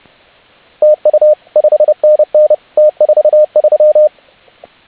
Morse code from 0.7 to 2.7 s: TU 5NN